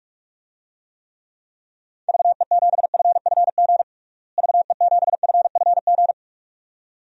Morse code, heard as a keyed tone, about 43 wpm, 700 Hz. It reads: VE7FFG VE7FFG
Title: DXCC-100 Call Signs - 6 Characters - Repeated 2x - 45wpm